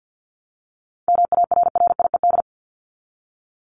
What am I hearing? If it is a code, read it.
Morse code, heard as a keyed tone, about 50 wpm, 700 Hz. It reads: MARRIED
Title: ED Suffix - Top 100 Words with Base Words 50wpm